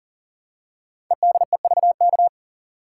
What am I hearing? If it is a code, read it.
Morse code, heard as a keyed tone, about 40 wpm, 700 Hz. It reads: EDEVK
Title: Code Groups - 5 Letters - 40wpm